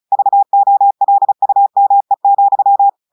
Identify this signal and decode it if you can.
Morse code, heard as a keyed tone, about 35 wpm, 800 Hz. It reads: VOLUME,
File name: audios / portugais / portugais texte en clair 1 vitesse 35 mots minutes